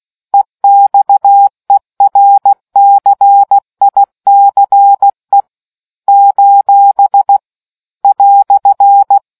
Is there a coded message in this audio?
Yes: Morse code from 0.3 to 9.2 s, heard as a keyed tone, about 16 wpm, 800 Hz. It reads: EXERCICE : "